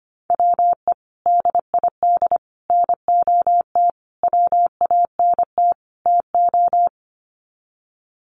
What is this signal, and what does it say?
Morse code, heard as a keyed tone, about 25 wpm, 700 Hz. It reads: WE DID NOT WANT TO
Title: Sets of 5 Words Encore 25wpm